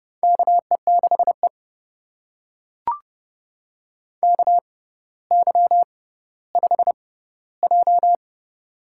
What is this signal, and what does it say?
Morse code, heard as a keyed tone, about 32 wpm, spaced as for 16 wpm, 700 Hz. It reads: KE6E KY5J
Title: USA Call Signs - 4 Characters - 6x Character Spacing - 30wpm